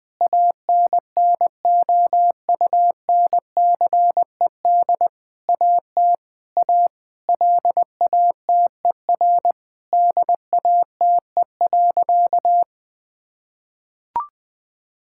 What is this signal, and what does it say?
Morse code, heard as a keyed tone, about 20 wpm, 700 Hz. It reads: ANNOUNCED AT A LATER DATE.
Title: Sentences with 7-Word Phrases - 20wpm